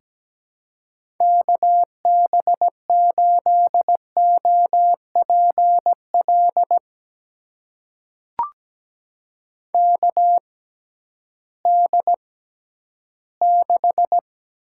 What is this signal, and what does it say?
Morse code, heard as a keyed tone, about 17 wpm, spaced as for 11 wpm, 700 Hz. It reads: KB8OPL KD6
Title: USA Call Signs - 6 Characters - 6x Character Spacing - 17wpm